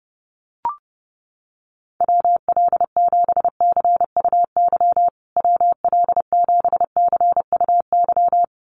Morse code, heard as a keyed tone, about 30 wpm, 700 Hz. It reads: WL7CUY WL7CUY
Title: USA Call Signs Repeated 2x 30wpm